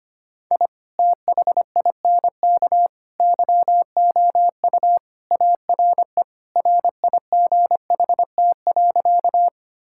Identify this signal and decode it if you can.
Morse code, heard as a keyed tone, about 25 wpm, 700 Hz. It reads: I THINK YOU ARE RIGHT.
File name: practice / 25wpm / Sentences with 3-Word Phrases - 25wpm